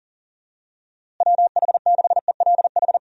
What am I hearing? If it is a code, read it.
Morse code, heard as a keyed tone, about 40 wpm, 700 Hz. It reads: WH6ELH
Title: USA Call Signs - 6 Characters - Repeated 2x - 40wpm